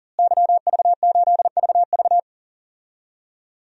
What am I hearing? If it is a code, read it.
Morse code, heard as a keyed tone, about 37 wpm, 700 Hz. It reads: YV8VV
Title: DXCC-100 Call Signs - 5 Characters - Repeated 2x - 40wpm